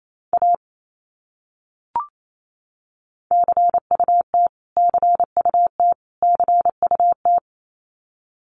Morse code, heard as a keyed tone, about 28 wpm, 700 Hz. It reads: A CUT CUT CUT
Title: Top 300 Words Repeated 28wpm